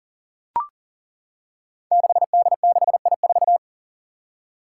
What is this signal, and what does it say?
Morse code, heard as a keyed tone, about 39 wpm, 700 Hz. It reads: 6D6I4
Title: Five Challenging Letter-Number Combinations 40wpm